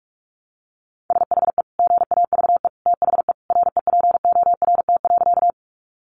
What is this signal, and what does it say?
Morse code, heard as a keyed tone, about 44 wpm, 700 Hz. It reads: SHE GAVE THE REPORT.
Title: Sentences from Top 900 Words 45wpm